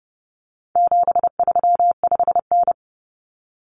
Morse code, heard as a keyed tone, about 30 wpm, 700 Hz. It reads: 735N
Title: Common Cut Numbers - 4-Digit 30wpm